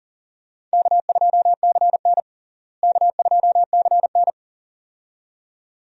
Morse code, heard as a keyed tone, about 40 wpm, 700 Hz. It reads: K2CN K2CN
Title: USA Call Signs - 4 Characters - Repeated 2x - 40wpm